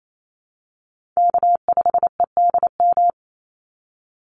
Morse code, heard as a keyed tone, about 28 wpm, 700 Hz. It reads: K5EDM
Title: Call Signs - 28wpm